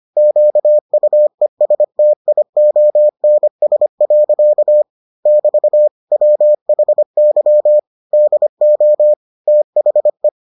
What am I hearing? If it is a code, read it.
Morse code, heard as a keyed tone, about 25 wpm, 600 Hz. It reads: QUESTIONS. <BT> WHY DO THE